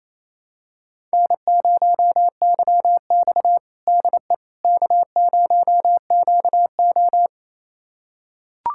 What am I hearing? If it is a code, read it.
Morse code, heard as a keyed tone, about 28 wpm, 700 Hz. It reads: N0YX DE K0QO
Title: USA Callsign DE Callsign - 4 Characters - 28wpm